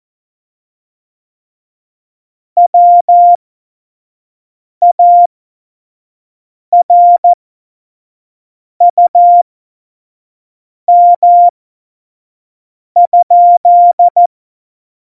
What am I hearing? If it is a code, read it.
Morse code, heard as a keyed tone, about 14 wpm, spaced as for 5 wpm, 700 Hz. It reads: WARUM?
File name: Morsekurs / Witze / Witz 11 Juli 17, Tempo 25 F70